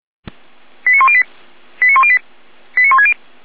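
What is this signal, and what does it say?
Five-tone selective call (ZVEI), heard as quick runs of five tones, tones of about 70 ms. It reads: 89108 89108 89160
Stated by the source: SDR-received ZVEI five-tone selective call.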